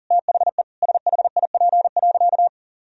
Morse code, heard as a keyed tone, about 40 wpm, 700 Hz. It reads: THE SHIP.
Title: Sentences from Top 1000 Words - Review Part 2 - 40wpm